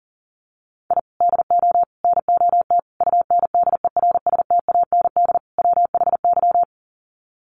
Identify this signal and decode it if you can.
Morse code, heard as a keyed tone, about 40 wpm, 700 Hz. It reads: I DO NOT UNDERSTAND WHY
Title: Sets of 5 Words Encore 40wpm